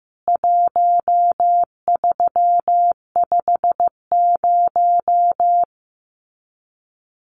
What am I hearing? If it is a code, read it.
Morse code, heard as a keyed tone, about 14 wpm, 700 Hz. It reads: 1350
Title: CWT Exchange 15wpm